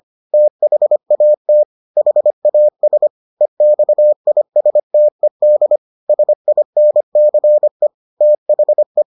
Morse code, heard as a keyed tone, about 25 wpm, 600 Hz. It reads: THAT HAS EXISTED SINCE THE